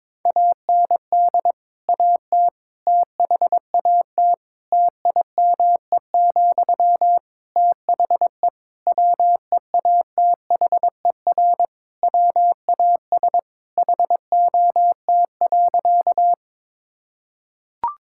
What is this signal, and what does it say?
Morse code, heard as a keyed tone, about 22 wpm, 700 Hz. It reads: AND AT THAT TIME, THE WEATHER WAS HOT.
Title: Sentences from Top 1000 Words - Review Part 5 - 22wpm